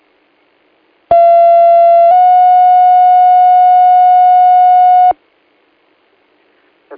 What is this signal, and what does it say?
Two-tone page, heard as a one-second tone, then a three-second tone, about 680 then 730 Hz. It is heard from 1.1 to 5.1 s.